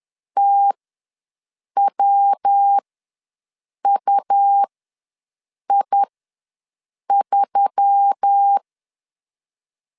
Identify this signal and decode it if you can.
Morse code, heard as a keyed tone, about 10 wpm, spaced as for 6 wpm, 800 Hz. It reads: TWUI3
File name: LEZIONI / 5 LBVC / 5 LB ws 10wpm 3616